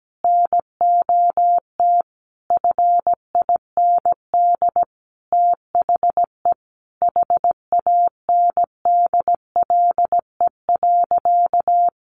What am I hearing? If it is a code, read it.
Morse code, heard as a keyed tone, about 17 wpm, 700 Hz. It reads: NOT FIND THE HANDLE.
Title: Sentences from Top 1100 Words 17wpm